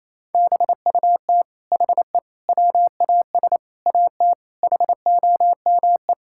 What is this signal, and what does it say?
Morse code, heard as a keyed tone, about 28 wpm, 700 Hz. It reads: BUT HE WAS AT HOME
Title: Sentences from Top 1000 Words - Review Part 5 - 28wpm